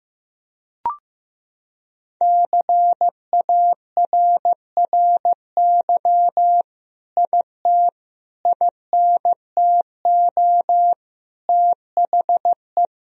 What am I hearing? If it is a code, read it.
Morse code, heard as a keyed tone, about 15 wpm, 700 Hz. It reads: CARRY IT INTO THE
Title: Sentences from Top 1000 Words - Review Part 1 - 15wpm